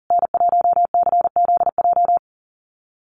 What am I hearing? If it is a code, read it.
Morse code, heard as a keyed tone, about 40 wpm, 700 Hz. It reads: N1CZJ
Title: USA Call Signs Repeated 3x 40wpm